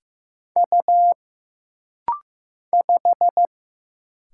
Morse code, heard as a keyed tone, about 15 wpm, 700 Hz. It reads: U 5
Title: Single Letter-Number Alt - Rapid-Fire 15wpm